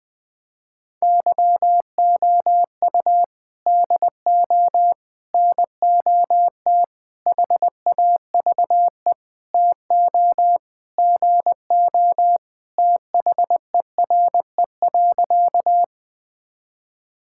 Morse code, heard as a keyed tone, about 20 wpm, 700 Hz. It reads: YOU DO NOT HAVE TO GO THERE.
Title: Sentences with 5-Word Phrases - 20wpm